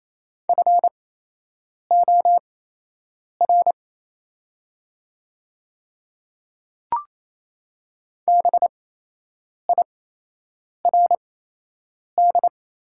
Morse code, heard as a keyed tone, about 28 wpm, spaced as for 8 wpm, 700 Hz. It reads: FOR BIRD
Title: 8x Character Spacing - Top 500 Words - 28wpm